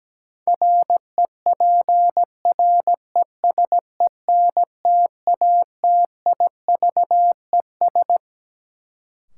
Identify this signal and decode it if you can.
Morse code, heard as a keyed tone, about 17 wpm, 700 Hz. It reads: REPRESENTATIVES
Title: Sets of 5 Words Encore - New Word Drill - 17wpm